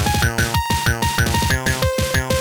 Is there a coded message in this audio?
No.